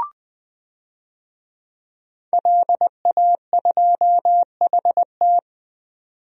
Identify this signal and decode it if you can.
Morse code, heard as a keyed tone, about 20 wpm, 700 Hz. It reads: LA2HT
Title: DXCC-100 Call Signs - 5 Characters - 20wpm